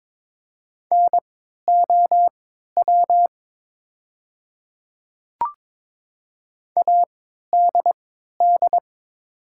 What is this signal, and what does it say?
Morse code, heard as a keyed tone, about 22 wpm, spaced as for 13 wpm, 700 Hz. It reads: NOW ADD
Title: 3x Character Spacing - Top 200 Words - 22wpm